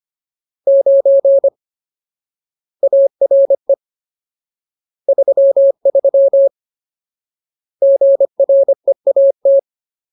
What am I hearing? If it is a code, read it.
Morse code, heard as a keyed tone, about 25 wpm, 550 Hz. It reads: 9 ARE 33 GREAT
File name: Ultimate Word List / 25WPM Ultimate